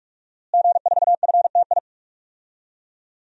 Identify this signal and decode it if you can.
Morse code, heard as a keyed tone, about 46 wpm, 700 Hz. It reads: G4FTI
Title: DXCC-100 Call Signs - 5 Characters - Repeated 3x - 45wpm